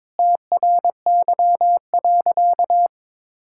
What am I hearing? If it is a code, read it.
Morse code, heard as a keyed tone, about 22 wpm, 700 Hz. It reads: TRY.
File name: practice / 22wpm / Sentences with 2-Word Phrases - 22wpm